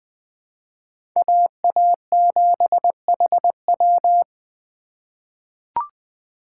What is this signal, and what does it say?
Morse code, heard as a keyed tone, about 21 wpm, 700 Hz. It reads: AA7HW
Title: USA Call Signs - 5 Characters - 4x Character Spacing - 20wpm